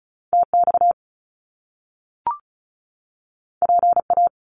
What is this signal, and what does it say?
Morse code, heard as a keyed tone, about 35 wpm, 700 Hz. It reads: TX PA
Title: US States 35wpm